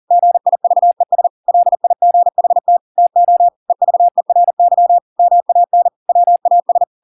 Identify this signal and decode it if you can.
Morse code, heard as a keyed tone, about 40 wpm, 700 Hz. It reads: GIVES LIGHT TO EVERY MAN WAS